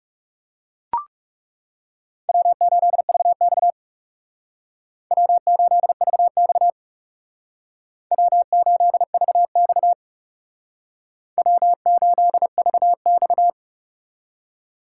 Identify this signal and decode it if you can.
Morse code, heard as a keyed tone, about 39 wpm, 700 Hz. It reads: W8VX W8VX W8VX W8VX